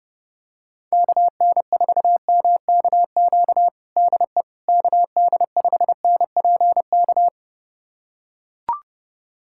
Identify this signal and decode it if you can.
Morse code, heard as a keyed tone, about 30 wpm, 700 Hz. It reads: KN4MKQ DE KD5NPK
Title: USA Callsign DE Callsign - 6 Characters - 30wpm